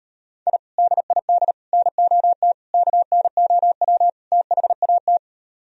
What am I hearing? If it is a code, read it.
Morse code, heard as a keyed tone, about 38 wpm, 700 Hz. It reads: I DID NOT KNOW THAT